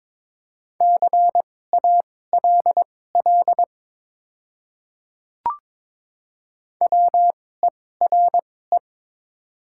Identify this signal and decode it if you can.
Morse code, heard as a keyed tone, about 22 wpm, spaced as for 16 wpm, 700 Hz. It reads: CALL WERE
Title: 2x Character Spacing - Top 500 Words - 22wpm